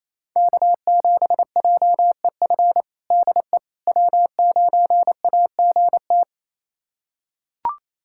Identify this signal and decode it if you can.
Morse code, heard as a keyed tone, about 28 wpm, 700 Hz. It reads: K7JEF DE W9AGT
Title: USA Callsign DE Callsign - 5 Characters - 28wpm